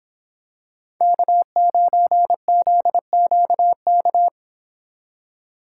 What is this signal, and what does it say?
Morse code, heard as a keyed tone, about 25 wpm, 700 Hz. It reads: K9ZQK